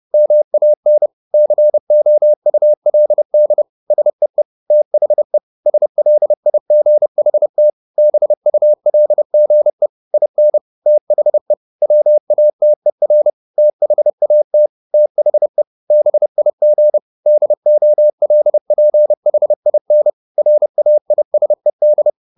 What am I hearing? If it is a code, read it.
Morse code, heard as a keyed tone, about 30 wpm, 600 Hz. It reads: MAN COULD SEE THE SLIGHT BULGE IN THE WATER THAT THE BIG DOLPHIN RAISED
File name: the old man and the sea / the old man and the sea 1 30wpm